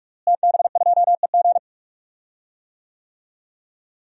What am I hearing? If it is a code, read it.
Morse code, heard as a keyed tone, about 46 wpm, 700 Hz. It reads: TB2EG
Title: DXCC-100 Call Signs - 5 Characters - Repeated 3x - 45wpm